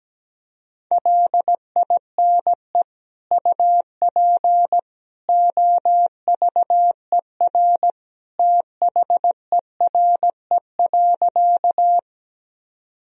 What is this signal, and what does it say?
Morse code, heard as a keyed tone, about 17 wpm, 700 Hz. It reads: LINE UP OVER THERE.
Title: Sentences from Top 1000 Words - Review Part 1 - 17wpm